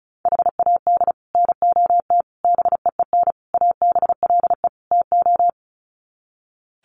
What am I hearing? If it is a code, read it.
Morse code, heard as a keyed tone, about 34 wpm, 700 Hz. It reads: HAD NOT BEEN ABLE TO
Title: Sets of 5 Words Encore 35wpm